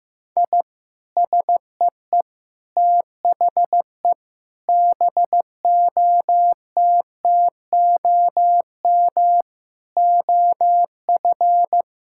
Morse code, heard as a keyed tone, about 15 wpm, 700 Hz. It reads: I SEE THE BOTTOM OF
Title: Sentences from Top 1000 Words - Review Part 6 - 15wpm